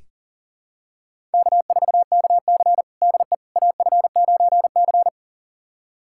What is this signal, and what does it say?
Morse code, heard as a keyed tone, about 40 wpm, 700 Hz. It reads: K4KC DE AF9C
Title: USA Callsign DE Callsign - 4 Characters - 40wpm